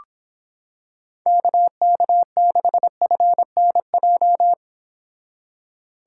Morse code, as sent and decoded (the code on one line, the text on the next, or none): -.- -.- -.... ..-. -. .---
KK6FNJ